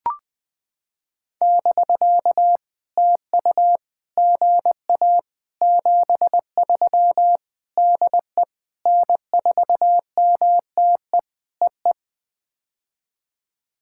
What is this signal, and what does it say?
Morse code, heard as a keyed tone, about 20 wpm, 700 Hz. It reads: <BK> TU GA 73 DE N4MTE EE
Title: POTA QSO Practice - 20wpm